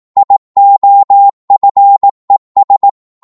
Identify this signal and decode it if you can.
Morse code, heard as a keyed tone, about 18 wpm, 800 Hz. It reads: IOFES
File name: Morse / ltr-18wpm